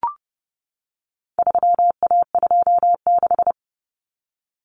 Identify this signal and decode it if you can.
Morse code, heard as a keyed tone, about 30 wpm, 700 Hz. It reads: 3A26
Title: Common Cut Numbers - 4-Digit 30wpm